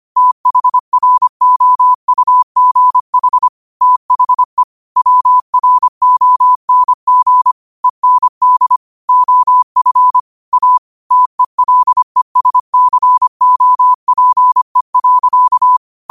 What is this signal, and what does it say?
Morse code, heard as a keyed tone, about 25 wpm, 1000 Hz. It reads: THROUGH THE WRONG END OF A TELESCOPE.
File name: qotdfiles / 25wpm